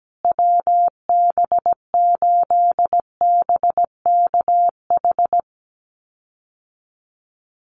Morse code, heard as a keyed tone, about 17 wpm, 700 Hz. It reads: WB8BKH
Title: USA Call Signs Repeated 3x 17wpm